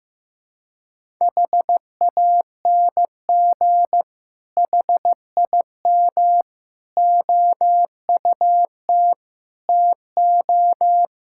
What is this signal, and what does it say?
Morse code, heard as a keyed tone, about 15 wpm, 700 Hz. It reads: HANG HIM OUT TO
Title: Sentences from Top 1600 Words 15wpm